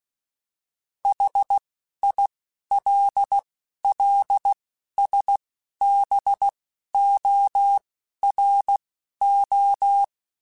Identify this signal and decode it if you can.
Morse code, heard as a keyed tone, about 16 wpm, spaced as for 11 wpm, 800 Hz. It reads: HILLSBORO
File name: fromSMs / FromSMs mp3s / Cities@10